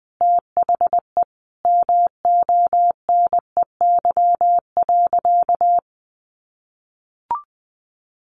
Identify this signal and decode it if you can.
Morse code, heard as a keyed tone, about 20 wpm, 700 Hz. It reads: THE MONEY.
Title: Sentences from Top 1400 Words 20wpm